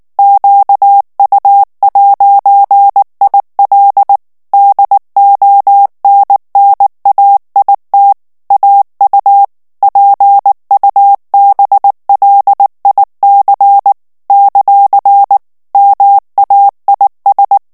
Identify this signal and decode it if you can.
Morse code, heard as a keyed tone, about 19 wpm, 800 Hz. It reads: QU'IL DONNAIT AU PUBLIC ; MAIS
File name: CW / FICHIERS-MP3 / FRANCAIS TEXTE CLAIR 2 VITESSE 18 MOTS MINUTES